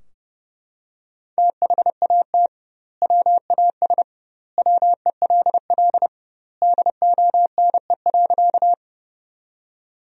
Morse code, heard as a keyed tone, about 32 wpm, 700 Hz. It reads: THAT WAS WELL DONE.
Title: 2x Word-Spacing - Sentences from Top 500 Words - 30wpm